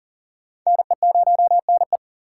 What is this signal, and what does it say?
Morse code, heard as a keyed tone, about 54 wpm, spaced as for 47 wpm, 700 Hz. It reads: NE0NE
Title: Speed Racing - USA Call Signs 40wpm